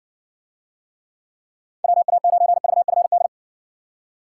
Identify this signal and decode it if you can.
Morse code, heard as a keyed tone, about 60 wpm, 700 Hz. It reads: WA9VFD